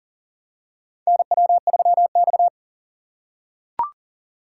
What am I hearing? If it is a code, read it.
Morse code, heard as a keyed tone, about 40 wpm, 700 Hz. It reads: NW3X
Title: USA Call Signs - 4 Characters - 3x Character Spacing - 40wpm